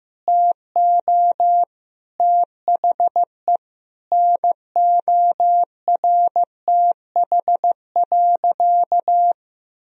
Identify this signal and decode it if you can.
Morse code, heard as a keyed tone, about 15 wpm, 700 Hz. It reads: TO THE NORTH.
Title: Sentences from Top 1000 Words - Review Part 2 - 15wpm